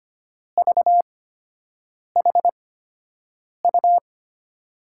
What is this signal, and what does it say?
Morse code, heard as a keyed tone, about 25 wpm, spaced as for 7 wpm, 700 Hz. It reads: VHU
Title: USA Call Signs - 8x Character Spacing - 25wpm